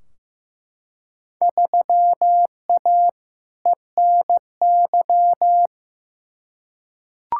Morse code, heard as a keyed tone, about 15 wpm, 700 Hz. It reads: 3A ENY
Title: Field Day Exchanges - 15wpm